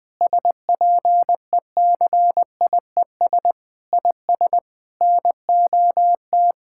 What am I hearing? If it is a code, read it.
Morse code, heard as a keyed tone, about 20 wpm, 700 Hz. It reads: SPECIES IS NOT
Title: Sentences with 6-Word Phrases - Spoken Beginning - 20wpm